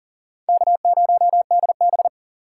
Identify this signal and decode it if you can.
Morse code, heard as a keyed tone, about 40 wpm, 700 Hz. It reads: K0DB
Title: USA Call Signs - 4 Characters - 3x Character Spacing - 40wpm